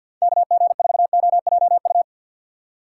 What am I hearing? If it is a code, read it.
Morse code, heard as a keyed tone, about 52 wpm, 700 Hz. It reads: KG4OJU
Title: USA Call Signs - 6 Characters - 3x Character Spacing - 50wpm